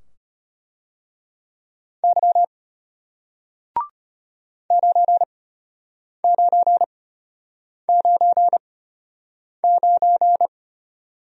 Morse code, heard as a keyed tone, about 33 wpm, spaced as for 8 wpm, 700 Hz. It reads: Y 9999